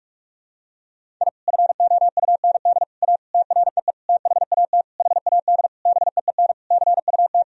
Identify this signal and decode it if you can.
Morse code, heard as a keyed tone, about 46 wpm, 700 Hz. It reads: I FOUND A TREE THAT HAD BEEN CUT
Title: Sentences from Top 1000 Words - Review Part 1 - 45wpm